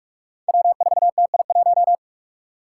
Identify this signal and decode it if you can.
Morse code, heard as a keyed tone, about 41 wpm, 700 Hz. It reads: W4TI1
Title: Five Challenging Letter-Number Combinations 45wpm